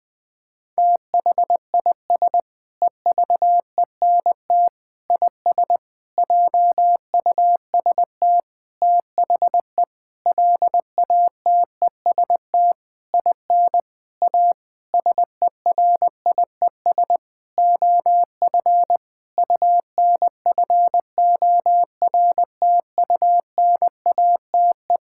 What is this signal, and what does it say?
Morse code, heard as a keyed tone, about 20 wpm, 700 Hz. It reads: THIS EVENT IS JUST THE LATEST IN A SERIES OF UNFORTUNATE
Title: Sentences with 8-Word Phrases - 20wpm